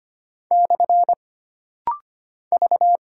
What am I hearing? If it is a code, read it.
Morse code, heard as a keyed tone, about 25 wpm, 700 Hz. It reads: / V
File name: Single Character (Letter:Number:Punctuation:Prosign) - Rapid-Fire - 25wpm